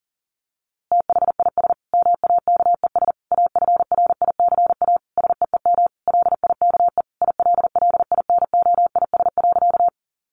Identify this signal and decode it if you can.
Morse code, heard as a keyed tone, about 40 wpm, 700 Hz. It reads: THIS MAKES AFRICA SEEM LIKE ILLINOIS.
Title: A Sound Of Thunder 40wpm